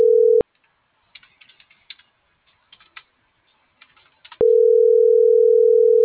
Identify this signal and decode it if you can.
Ringback tone (North America), 440 + 480 Hz.